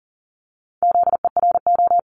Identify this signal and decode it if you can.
Morse code, heard as a keyed tone, about 41 wpm, 700 Hz. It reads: ZERO
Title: Multi-letter Phonemes 40wpm